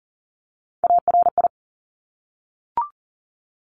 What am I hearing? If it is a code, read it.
Morse code, heard as a keyed tone, about 40 wpm, 690 Hz. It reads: ARI
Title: Top Three Letter Combinations in Words 40wpm